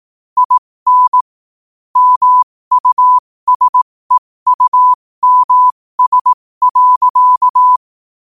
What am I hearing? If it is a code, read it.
Morse code, heard as a keyed tone, about 18 wpm, spaced as for 15 wpm, 1000 Hz. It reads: IN MUSEUMS.
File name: qotdfiles / 15wpm